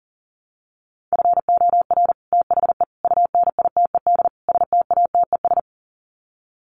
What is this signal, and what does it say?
Morse code, heard as a keyed tone, about 40 wpm, 700 Hz. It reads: FOR THE UNITED STATES
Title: Sets of 4 Words 40wpm